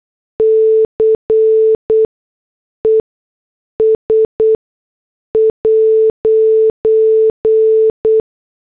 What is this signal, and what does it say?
Morse code, heard as a keyed tone, about 8 wpm, spaced as for 6 wpm, 440 Hz. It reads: CES'